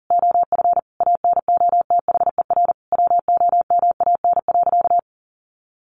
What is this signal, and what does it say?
Morse code, heard as a keyed tone, about 40 wpm, 700 Hz. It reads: OF ANOTHER WOMAN.
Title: Sentences from Top 1400 Words 40wpm